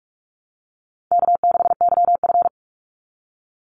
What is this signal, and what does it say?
Morse code, heard as a keyed tone, about 44 wpm, 700 Hz. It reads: K6YF
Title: Easy Call Signs - 45wpm